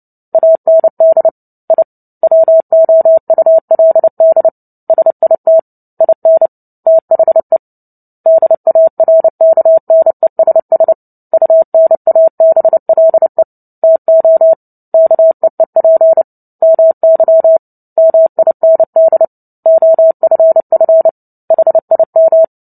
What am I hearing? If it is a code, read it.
Morse code, heard as a keyed tone, about 30 wpm, 650 Hz. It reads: AND I WOULD SIT IN THE DARKNESS UNABLE TO KEEP MY MIND OFF HIM
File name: Book Two / 03-Chapter 3 - 29 WPM